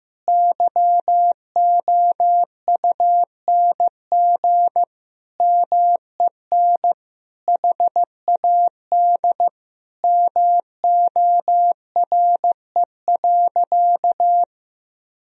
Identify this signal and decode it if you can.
Morse code, heard as a keyed tone, about 15 wpm, 700 Hz. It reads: YOUNG MEN HAD MORE.
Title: Sentences from Top 1000 Words - Review Part 1 - 15wpm